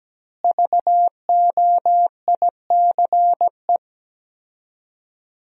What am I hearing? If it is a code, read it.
Morse code, heard as a keyed tone, about 17 wpm, 700 Hz. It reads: VOICE
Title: 4x Word-Spacing - Sentences from Top 500 Words - 17wpm